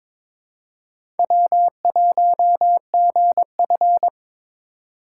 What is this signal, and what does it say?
Morse code, heard as a keyed tone, about 22 wpm, 690 Hz. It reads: W1GF
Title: USA Call Signs - 4 Characters - 3x Character Spacing - 22wpm